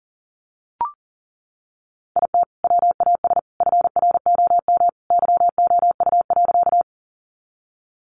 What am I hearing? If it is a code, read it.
Morse code, heard as a keyed tone, about 40 wpm, 700 Hz. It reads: IT WAS FROM YOU.
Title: Sentences from Top 100 Words 40wpm